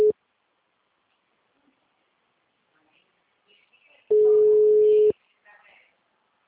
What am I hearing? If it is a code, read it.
Ringback tone (Europe, 425 Hz), with speech.